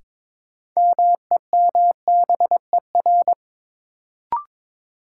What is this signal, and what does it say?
Morse code, heard as a keyed tone, about 22 wpm, 700 Hz. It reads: MEMBER
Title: Sets of 3 Words - New Word Drill - 22wpm